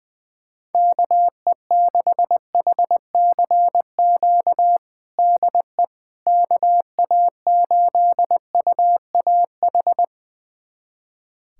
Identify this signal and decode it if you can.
Morse code, heard as a keyed tone, about 20 wpm, 700 Hz. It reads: KE6HCQ DE KA8UAH
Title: USA Callsign DE Callsign 20wpm